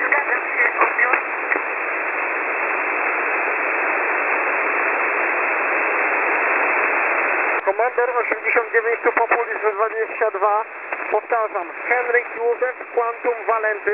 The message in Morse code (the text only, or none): none